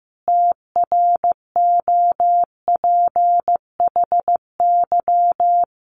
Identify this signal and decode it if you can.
Morse code, heard as a keyed tone, about 15 wpm, 700 Hz. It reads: TROPHY
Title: A Sound Of Thunder 15wpm